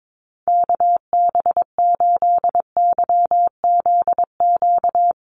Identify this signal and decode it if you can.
Morse code, heard as a keyed tone, about 22 wpm, 700 Hz. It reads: KB8YZQ
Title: USA Call Signs 22wpm